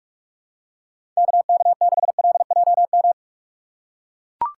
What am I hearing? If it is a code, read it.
Morse code, heard as a keyed tone, about 44 wpm, 700 Hz. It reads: KK6LJM